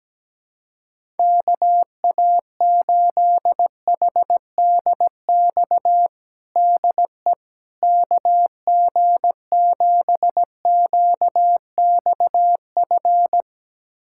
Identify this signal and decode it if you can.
Morse code, heard as a keyed tone, about 17 wpm, 700 Hz. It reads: KA8HDX DE KG7QXF
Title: USA Callsign DE Callsign - 6 Characters - 17wpm